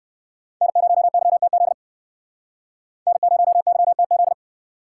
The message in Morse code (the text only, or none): N0YTZ N0YTZ